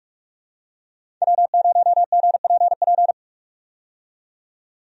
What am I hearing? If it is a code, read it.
Morse code, heard as a keyed tone, about 46 wpm, 700 Hz. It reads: W0GPP
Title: USA Call Signs - 5 Characters - 45wpm